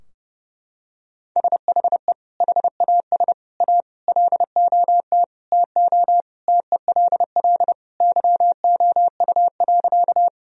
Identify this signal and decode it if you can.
Morse code, heard as a keyed tone, about 30 wpm, 700 Hz. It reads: SHE HAS A LOT TO TELL YOU.
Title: Sentences from Top 1000 Words - Review Part 3 - 30wpm